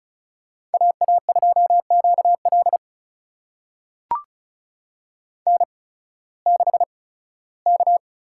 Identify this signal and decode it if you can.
Morse code, heard as a keyed tone, about 36 wpm, spaced as for 19 wpm, 690 Hz. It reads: AA2QL N6K
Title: USA Call Signs - 5 Characters - 8x Character Spacing - 35wpm